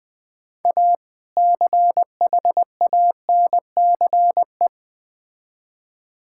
Morse code, heard as a keyed tone, about 20 wpm, 700 Hz. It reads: A CHANCE
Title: Build A Phrase - 3-Word Phrases - 20wpm